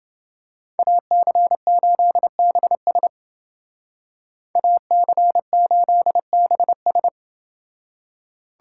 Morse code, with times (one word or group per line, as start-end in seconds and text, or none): 0.8-3.1 AC8BS
4.6-7.1 AC8BS